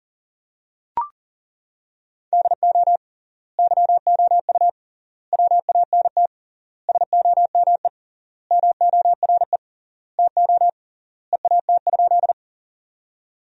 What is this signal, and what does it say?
Morse code, heard as a keyed tone, about 40 wpm, 700 Hz. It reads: DO YOU WANT SOME MORE TO EAT?
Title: 3x Word-Spacing - Sentences from Top 300 Words - 40wpm